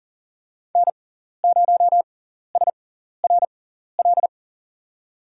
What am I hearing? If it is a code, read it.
Morse code, heard as a keyed tone, about 40 wpm, spaced as for 14 wpm, 700 Hz. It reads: N0SRL
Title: USA Call Signs - 5 Characters - 6x Character Spacing - 40wpm